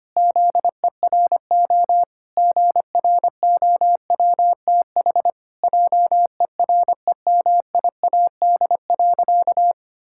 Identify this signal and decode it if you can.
Morse code, heard as a keyed tone, about 25 wpm, 700 Hz. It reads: ZERO GROWTH JEREMIAD.
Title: Pangram Challenge v2 - 25wpm